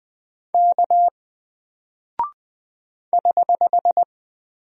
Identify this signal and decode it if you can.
Morse code, heard as a keyed tone, about 20 wpm, 700 Hz. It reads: K <HH>
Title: Single Character (Letter:Number:Punctuation:Prosign) 20wpm